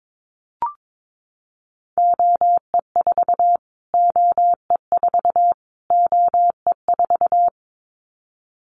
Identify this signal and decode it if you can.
Morse code, heard as a keyed tone, about 22 wpm, 700 Hz. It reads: OE4 OE4 OE4
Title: Callsign Prefixes for Top 25 DXCC - Repeated 3x - 22wpm